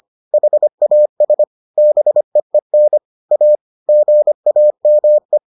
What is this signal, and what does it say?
Morse code, heard as a keyed tone, about 25 wpm, 600 Hz. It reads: HAS BEEN A GAME